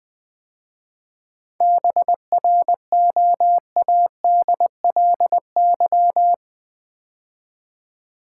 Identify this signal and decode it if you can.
Morse code, heard as a keyed tone, about 20 wpm, 700 Hz. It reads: BROADLY
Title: 7 Letter Words - First 3 Letters Given - 20wpm